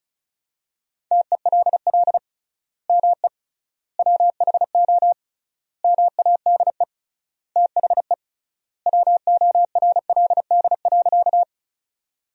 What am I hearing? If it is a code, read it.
Morse code, heard as a keyed tone, about 36 wpm, 700 Hz. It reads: TELL ME WHO MADE THE WORLD.
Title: 3x Word-Spacing - Sentences from Top 200 Words - 35wpm